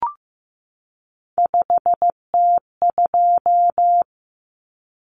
Morse code, heard as a keyed tone, about 15 wpm, 700 Hz. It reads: 5T2
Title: Common Cut Numbers - 3-Digit 15wpm